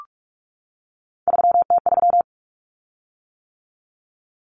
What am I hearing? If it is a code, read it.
Morse code, heard as a keyed tone, about 44 wpm, 700 Hz. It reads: 3T3
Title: Common Cut Numbers - 3-Digit 45wpm